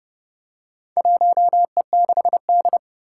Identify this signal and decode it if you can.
Morse code, heard as a keyed tone, about 29 wpm, 700 Hz. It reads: 1E6D
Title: Four Challenging Letter-Number Combinations 30wpm